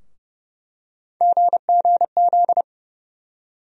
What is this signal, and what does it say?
Morse code, heard as a keyed tone, about 30 wpm, 700 Hz. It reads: GGZ
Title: USA Call Sign Postfixes - 3 Letters 30wpm